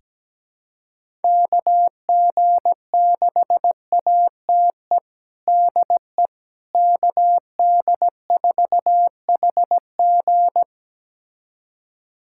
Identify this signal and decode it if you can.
Morse code, heard as a keyed tone, about 17 wpm, 700 Hz. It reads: KG6ATE DE KD4HG
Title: USA Callsign DE Callsign 17wpm